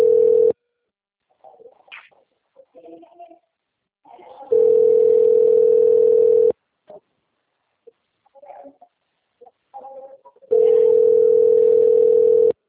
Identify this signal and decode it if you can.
Ringback tone (North America), 440 + 480 Hz.